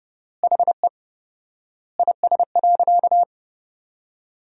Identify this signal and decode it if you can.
Morse code, heard as a keyed tone, about 30 wpm, 700 Hz. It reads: HE IS.
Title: 4x Word-Spacing - Sentences from Top 200 Words - 30wpm